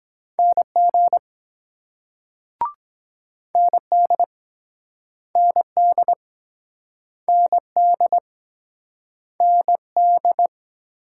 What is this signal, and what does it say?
Morse code, heard as a keyed tone, about 22 wpm, 700 Hz. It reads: NG ND ND ND ND